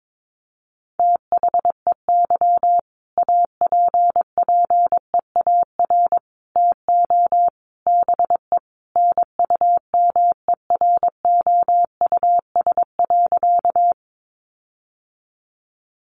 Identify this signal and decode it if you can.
Morse code, heard as a keyed tone, about 22 wpm, 700 Hz. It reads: THEY APPEAR TO BE NUMEROUS.
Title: Sentences from Top 1400 Words 22wpm